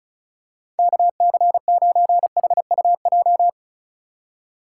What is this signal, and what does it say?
Morse code, heard as a keyed tone, about 36 wpm, 700 Hz. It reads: KC9HUJ
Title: USA Call Signs - 4x Character Spacing - 35wpm